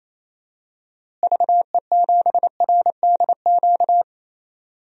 Morse code, heard as a keyed tone, about 28 wpm, 700 Hz. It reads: VE7RDQ
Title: DXCC-100 Call Signs - 6 Characters - Repeated 3x - 28wpm